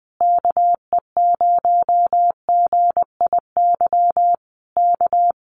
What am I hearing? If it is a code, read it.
Morse code, heard as a keyed tone, about 20 wpm, 700 Hz. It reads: KE0GIY K
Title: Common QSO Phrases Encore 20wpm